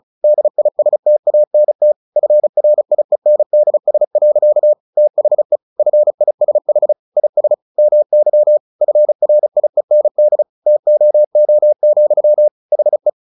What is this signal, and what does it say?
Morse code, heard as a keyed tone, about 34 wpm, 600 Hz. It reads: DISTANT FRIENDS. THE FISH IS MY FRIEND TOO, HE
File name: the old man and the sea 2 35wpm